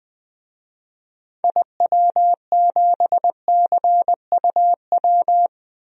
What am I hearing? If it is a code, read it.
Morse code, heard as a keyed tone, about 20 wpm, 700 Hz. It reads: IW7CUW
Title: DXCC-100 Call Signs - 6 Characters - 20wpm